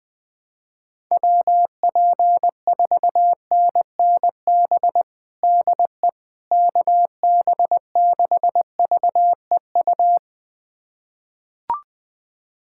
Morse code, heard as a keyed tone, about 20 wpm, 700 Hz. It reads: WP4NNB DE KB6VEU
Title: USA Callsign DE Callsign - 6 Characters - 20wpm